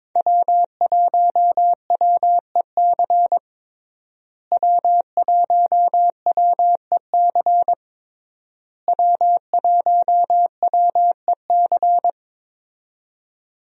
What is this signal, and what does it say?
Morse code, heard as a keyed tone, about 22 wpm, 700 Hz. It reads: W1WEC W1WEC W1WEC
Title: USA Call Signs - 5 Characters - Repeated 3x - 22wpm